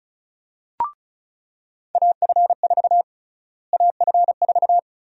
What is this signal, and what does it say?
Morse code, heard as a keyed tone, about 36 wpm, 700 Hz. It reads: AF4 AF4
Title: USA Call Sign Prefixes - Repeated 2x - 35wpm